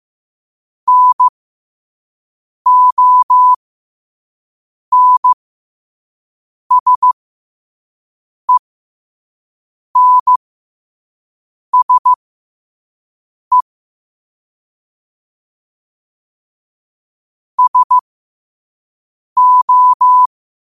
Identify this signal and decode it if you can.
Morse code, heard as a keyed tone, about 15 wpm, spaced as for 5 wpm, 1000 Hz. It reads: NONSENSE SO
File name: 5wpm